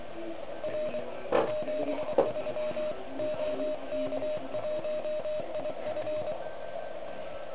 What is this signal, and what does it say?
Morse code, heard as a keyed tone, about 24 wpm, 610 Hz, with radio noise. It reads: R 73 OK1IF
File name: ARCHIV / OK QRP 2021 / REC03